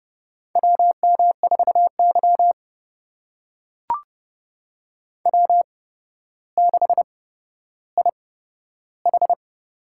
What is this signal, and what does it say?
Morse code, heard as a keyed tone, about 31 wpm, spaced as for 13 wpm, 700 Hz. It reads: WM4Y W6IH
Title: USA Call Signs - 8x Character Spacing - 30wpm